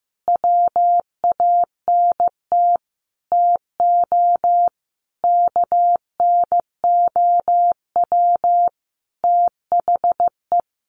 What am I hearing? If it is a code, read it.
Morse code, heard as a keyed tone, about 15 wpm, 700 Hz. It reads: WANT TO KNOW THE
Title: Sentences from Top 800 Words 15wpm